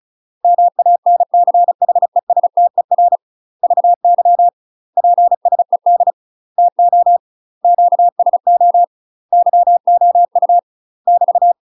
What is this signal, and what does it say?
Morse code, heard as a keyed tone, about 34 wpm, 700 Hz. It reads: MANCHESTER VY PSED TO QSO YOU <BT>
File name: UK MASTER QSOs TRAINING / 35WPM / UK MASTER QSOs 0025